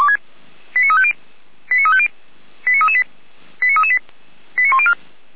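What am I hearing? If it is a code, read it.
Five-tone selective call (ZVEI), heard as quick runs of five tones, tones of about 70 ms. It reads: 89350 89350 89338 89338 89104
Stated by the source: ZVEI five-tone call 89164 received by SDR, archived by Feuerwehr Orlamünde fire brigade.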